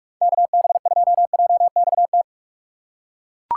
Morse code, heard as a keyed tone, about 41 wpm, 700 Hz. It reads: KB2JXT
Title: USA Call Signs - 6 Characters - 2x Character Spacing - 45wpm